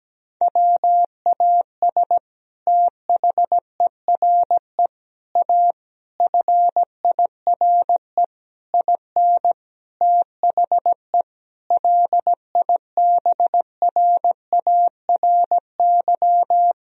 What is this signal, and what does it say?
Morse code, heard as a keyed tone, about 17 wpm, 700 Hz. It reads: WAS THERE A FIRE IN THE LIBRARY
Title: Sentences from Top 1000 Words - Review Part 4 - 17wpm